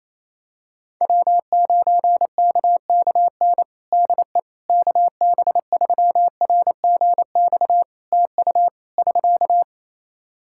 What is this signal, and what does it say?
Morse code, heard as a keyed tone, about 28 wpm, 700 Hz. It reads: W9KKN DE KB3RGX TU <SK>
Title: Practice QSO - Exchange 4 - Ending - 28wpm